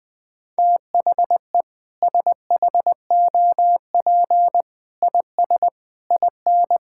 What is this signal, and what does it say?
Morse code, heard as a keyed tone, about 21 wpm, 700 Hz. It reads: THE SHOP IS IN
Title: Sentences with 4-Word Phrases - 20wpm